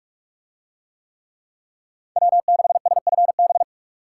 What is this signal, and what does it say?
Morse code, heard as a keyed tone, about 42 wpm, 700 Hz. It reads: W6SFB